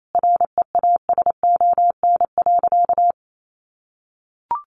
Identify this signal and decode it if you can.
Morse code, heard as a keyed tone, about 28 wpm, 700 Hz. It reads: REASON.
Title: Sentences from Top 700 Words 28wpm